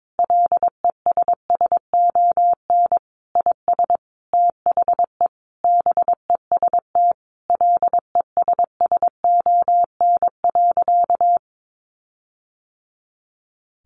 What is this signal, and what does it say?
Morse code, heard as a keyed tone, about 22 wpm, 700 Hz. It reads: LESSON IS THE BEST LESSON.
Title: Sentences from Top 1400 Words 22wpm